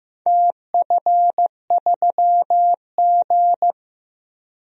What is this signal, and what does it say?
Morse code, heard as a keyed tone, about 16 wpm, 700 Hz. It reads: TF3G
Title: DXCC-100 Call Signs - 4 Characters - Repeated 2x - 15wpm